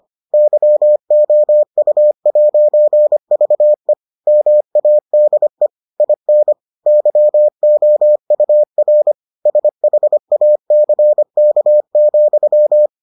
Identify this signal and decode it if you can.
Morse code, heard as a keyed tone, about 25 wpm, 600 Hz. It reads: YOU'VE MADE IN YOUR SHACK,